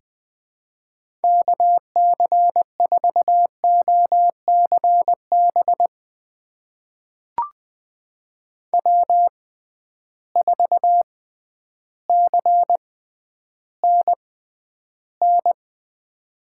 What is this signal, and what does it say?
Morse code, heard as a keyed tone, about 20 wpm, spaced as for 11 wpm, 700 Hz. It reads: KC4OCB W4CNN
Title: USA Call Signs - 6x Character Spacing - 20wpm